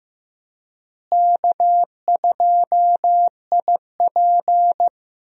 Morse code, heard as a keyed tone, about 15 wpm, 700 Hz. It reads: K2IP
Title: USA Call Signs - 4 Characters - Repeated 3x - 15wpm